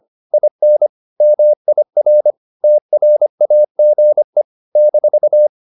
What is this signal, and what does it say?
Morse code, heard as a keyed tone, about 25 wpm, 600 Hz. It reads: IN MIR TRAGE -